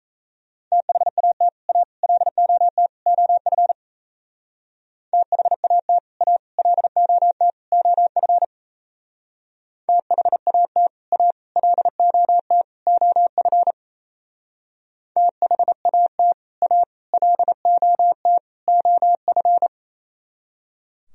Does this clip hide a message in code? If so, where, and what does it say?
Yes: Morse code from 0.7 to 19.7 s, heard as a keyed tone, about 36 wpm, 700 Hz. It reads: THAT A LOT OF THAT A LOT OF THAT A LOT OF THAT A LOT OF